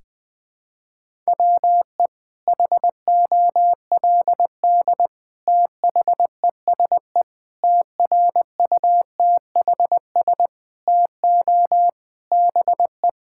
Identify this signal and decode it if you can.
Morse code, heard as a keyed tone, about 21 wpm, 700 Hz. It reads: WE HOLD THESE TRUTHS TO BE
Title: Sets of 9 Words - 20wpm